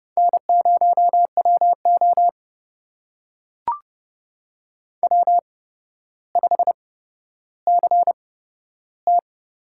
Morse code, heard as a keyed tone, about 30 wpm, spaced as for 13 wpm, 700 Hz. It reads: N0WO W5CT
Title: USA Call Signs - 4 Characters - 8x Character Spacing - 30wpm